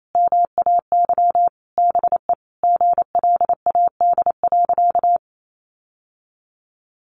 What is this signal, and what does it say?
Morse code, heard as a keyed tone, about 28 wpm, 700 Hz. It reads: MAY BE GLAD.
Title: Sentences from Top 800 Words 28wpm